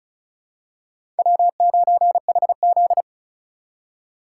Morse code, heard as a keyed tone, about 36 wpm, 700 Hz. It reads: W9HZ
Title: USA Call Signs - 4 Characters - 6x Character Spacing - 35wpm